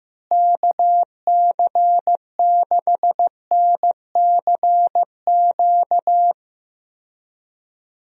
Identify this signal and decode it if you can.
Morse code, heard as a keyed tone, about 15 wpm, 700 Hz. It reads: KC6NCQ
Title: USA Call Signs - 6 Characters - Repeated 2x - 15wpm